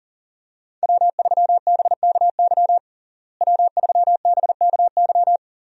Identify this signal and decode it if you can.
Morse code, heard as a keyed tone, about 40 wpm, 700 Hz. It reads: W3BKY W3BKY
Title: USA Call Signs - 5 Characters - Repeated 2x - 40wpm